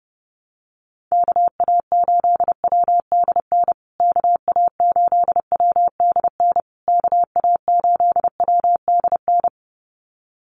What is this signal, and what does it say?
Morse code, heard as a keyed tone, about 28 wpm, 700 Hz. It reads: KA8WDN KA8WDN KA8WDN
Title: USA Call Signs Repeated 3x 30wpm